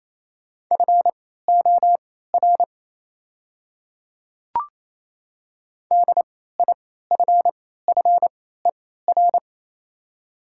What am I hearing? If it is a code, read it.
Morse code, heard as a keyed tone, about 28 wpm, spaced as for 16 wpm, 700 Hz. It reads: FOR DIFFER
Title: 3x Character Spacing - Top 200 Words - 28wpm